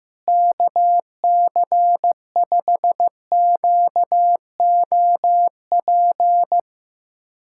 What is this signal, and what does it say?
Morse code, heard as a keyed tone, about 15 wpm, 700 Hz. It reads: KC5QOP
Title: USA Callsign DE Callsign 15wpm